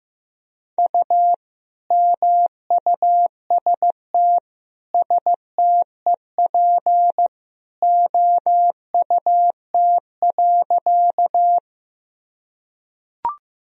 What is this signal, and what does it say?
Morse code, heard as a keyed tone, about 15 wpm, 700 Hz. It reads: U MUST STEP OUT.
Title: Sentences from Top 400 Words and Abbreviations 15wpm